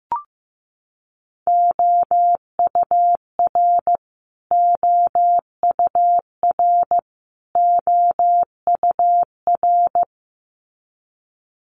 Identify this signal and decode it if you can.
Morse code, heard as a keyed tone, about 15 wpm, 700 Hz. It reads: OUR OUR OUR
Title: Top 500 Words Repeated 15wpm